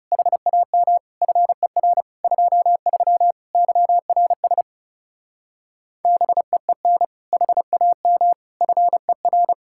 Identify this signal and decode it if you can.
Morse code, heard as a keyed tone, about 36 wpm, 700 Hz. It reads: HAM FER 23 YRS BEEN HAM FER
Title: Speed-Racing Common QSO Phrases - 30wpm